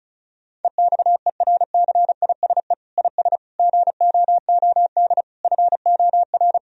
Morse code, heard as a keyed tone, about 34 wpm, 700 Hz. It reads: EXERCISE IS GOOD FOR
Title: Sentences from Top 1000 Words - Review Part 4 - 35wpm